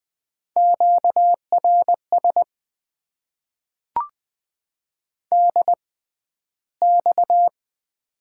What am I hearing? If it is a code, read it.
Morse code, heard as a keyed tone, about 20 wpm, spaced as for 12 wpm, 700 Hz. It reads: QRS DX
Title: Top 90 QSO Elements - 6x Character Spacing - 20wpm